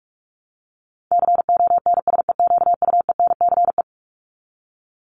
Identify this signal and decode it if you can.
Morse code, heard as a keyed tone, about 44 wpm, 700 Hz. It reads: CONSEQUENCE
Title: LY Suffix - Top 100 Words with Base Words 45wpm